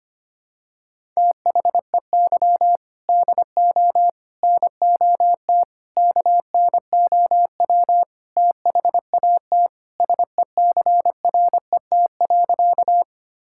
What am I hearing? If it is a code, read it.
Morse code, heard as a keyed tone, about 25 wpm, 700 Hz. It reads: THEY DO NOT KNOW THAT SECRET.
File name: practice / 25wpm / Sentences with 4-Word Phrases - 25wpm